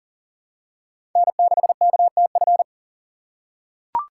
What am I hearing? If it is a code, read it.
Morse code, heard as a keyed tone, about 40 wpm, 700 Hz. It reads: N6KTF
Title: USA Call Signs - 5 Characters - 2x Character Spacing - 40wpm